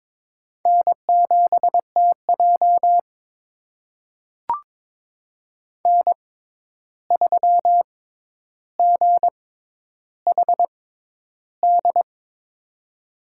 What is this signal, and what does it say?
Morse code, heard as a keyed tone, about 22 wpm, spaced as for 11 wpm, 700 Hz. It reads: N7TJ N3GHD
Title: USA Call Signs - 6x Character Spacing - 22wpm